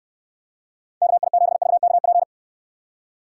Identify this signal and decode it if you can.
Morse code, heard as a keyed tone, about 69 wpm, 700 Hz. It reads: KI7VKP